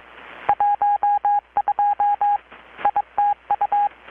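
Morse code, heard as a keyed tone, about 22 wpm, 820 Hz. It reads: 12 ITU
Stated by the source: Morse from KPH's Night of Nights XX maritime radiotelegraphy event, received in Chiapas, Mexico about 3,800 km away on a Kenwood R-600 with a grounded 10 m long-wire antenna.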